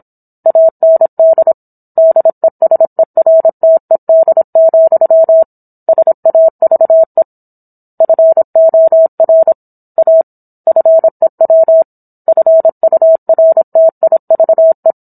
Morse code, heard as a keyed tone, about 27 wpm, 650 Hz. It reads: AND DESERTED, SAVE FOR A FEW FURTIVE